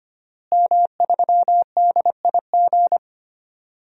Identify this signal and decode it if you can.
Morse code, heard as a keyed tone, about 25 wpm, 700 Hz. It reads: M3DIG
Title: DXCC-100 Call Signs - 5 Characters - Repeated 3x - 25wpm